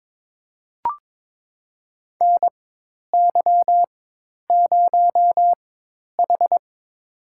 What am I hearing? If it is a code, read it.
Morse code, heard as a keyed tone, about 22 wpm, spaced as for 10 wpm, 700 Hz. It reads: NY0H
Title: USA Call Signs - 4 Characters - 4x Character Spacing - 22wpm